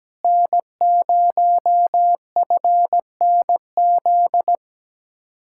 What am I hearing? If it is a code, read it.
Morse code, heard as a keyed tone, about 17 wpm, 700 Hz. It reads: N0FNZ
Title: Speed-Racing Common QSO Phrases - 17wpm